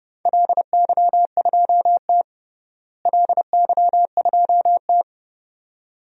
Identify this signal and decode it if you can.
Morse code, heard as a keyed tone, about 30 wpm, 700 Hz. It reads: LY2T LY2T
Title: DXCC-100 Call Signs - 4 Characters - Repeated 2x - 30wpm